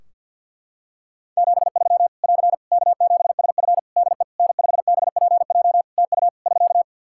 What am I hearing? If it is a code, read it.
Morse code, heard as a keyed tone, about 50 wpm, 700 Hz. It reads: 73 <AR> K7SF DE N5BPJ TU <SK>
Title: Practice QSO - Exchange 4 - Ending - 50wpm